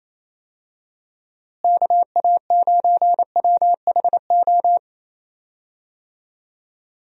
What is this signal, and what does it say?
Morse code, heard as a keyed tone, about 27 wpm, 700 Hz. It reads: KA9WHO
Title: USA Call Signs - 6 Characters - Repeated 2x - 28wpm